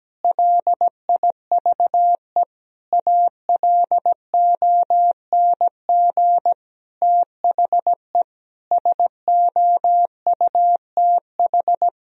Morse code, heard as a keyed tone, about 17 wpm, 700 Hz. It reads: LIVE ALONG THE SOUTH
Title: Sentences from Top 1000 Words - Review Part 4 - 17wpm